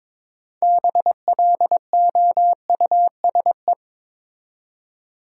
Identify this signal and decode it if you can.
Morse code, heard as a keyed tone, about 22 wpm, 700 Hz. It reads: BLOUSE
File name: Difficult Words with Hints - 22wpm